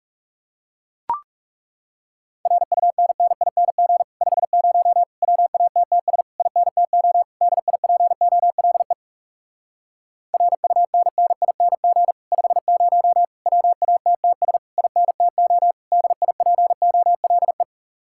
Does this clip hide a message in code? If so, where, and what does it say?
Yes: Morse code from 2.5 to 17.6 s, heard as a keyed tone, about 40 wpm, 700 Hz. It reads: RUNNING 50 WATTS INTO DIPOLE RUNNING 50 WATTS INTO DIPOLE